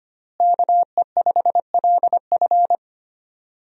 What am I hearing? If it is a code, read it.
Morse code, heard as a keyed tone, about 25 wpm, 700 Hz. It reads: KE5LF
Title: USA Call Signs - 5 Characters - Repeated 2x - 25wpm